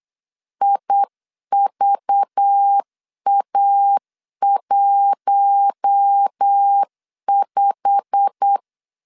Morse code, heard as a keyed tone, about 8 wpm, 800 Hz. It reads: IVA15
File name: CW / DET / 1 9 1500C S A 8wpm 5637